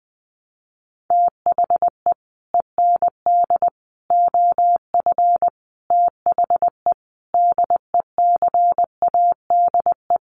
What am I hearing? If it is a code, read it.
Morse code, heard as a keyed tone, about 20 wpm, 700 Hz. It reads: THE END OF THE DECADE
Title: Sets of 5 Words Encore 20wpm